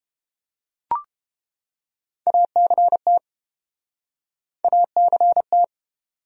Morse code, heard as a keyed tone, about 29 wpm, 700 Hz. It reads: ACT ACT